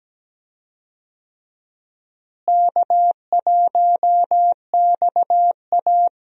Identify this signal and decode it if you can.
Morse code, heard as a keyed tone, about 17 wpm, 700 Hz. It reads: K1XA
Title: USA Call Signs - 4 Characters - Repeated 2x - 17wpm